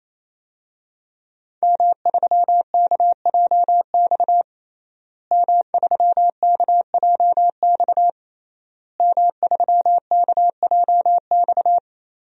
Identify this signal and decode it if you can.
Morse code, heard as a keyed tone, about 28 wpm, 700 Hz. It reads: M3KJX M3KJX M3KJX
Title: DXCC-100 Call Signs - 5 Characters - Repeated 3x - 28wpm